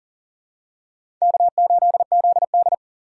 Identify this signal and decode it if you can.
Morse code, heard as a keyed tone, about 40 wpm, 700 Hz. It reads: K8ZD
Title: USA Call Signs - 4x Character Spacing - 40wpm